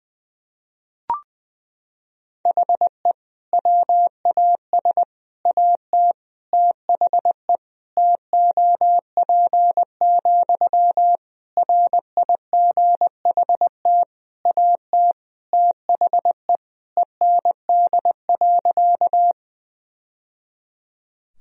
Morse code, heard as a keyed tone, about 21 wpm, 700 Hz. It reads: HE WAS AT THE TOP, RIGHT AT THE END.
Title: Sentences from Top 1000 Words - Review Part 2 - 20wpm